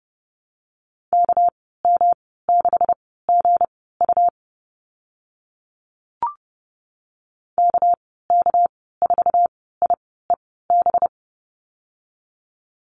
Morse code, heard as a keyed tone, about 29 wpm, spaced as for 17 wpm, 700 Hz. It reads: KM6GU KK4IEB
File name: Lesson 37g - Letter x - Call Signs 30wpm